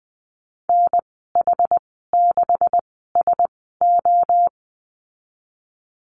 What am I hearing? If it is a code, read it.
Morse code, heard as a keyed tone, about 20 wpm, spaced as for 15 wpm, 700 Hz. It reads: NH6SO
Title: 154 Lesson 25g - Letter p - Call Signs 20wpm